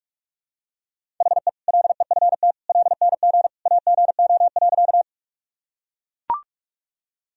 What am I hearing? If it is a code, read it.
Morse code, heard as a keyed tone, about 41 wpm, 700 Hz. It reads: HE LEFT LNG AGO.
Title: Sentences from Top 500 Words and Abbreviations 45wpm